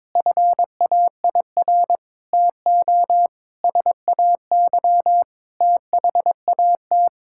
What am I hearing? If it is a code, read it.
Morse code, heard as a keyed tone, about 22 wpm, 700 Hz. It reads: FAIR TO SAY THAT
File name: practice / 22wpm / Sets of 7 Words - 22wpm